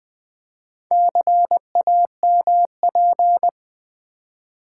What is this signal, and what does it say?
Morse code, heard as a keyed tone, about 20 wpm, 700 Hz. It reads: CAMP
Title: Build A Word - 2 Letters At A Time - 4-Letter Words - 20wpm